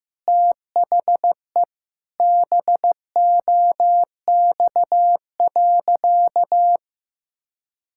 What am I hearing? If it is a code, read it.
Morse code, heard as a keyed tone, about 15 wpm, 700 Hz. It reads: THE BOX.
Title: Sentences from Top 1000 Words - Review Part 6 - 15wpm